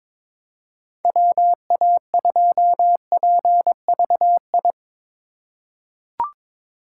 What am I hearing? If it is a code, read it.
Morse code, heard as a keyed tone, about 22 wpm, 700 Hz. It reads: WA2PVI
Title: USA Call Signs - 6 Characters - 4x Character Spacing - 22wpm